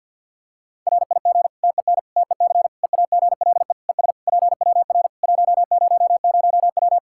Morse code, heard as a keyed tone, about 53 wpm, 700 Hz. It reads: RIG TEN TEC EAGLE ES PWR 100W
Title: Practice QSO - Exchange 2 - Info - 50wpm